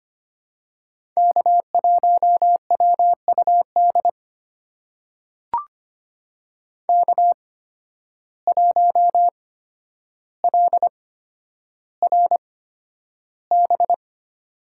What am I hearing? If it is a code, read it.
Morse code, heard as a keyed tone, about 25 wpm, spaced as for 11 wpm, 700 Hz. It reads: K1WUD K1LRB
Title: USA Call Signs - 5 Characters - 8x Character Spacing - 25wpm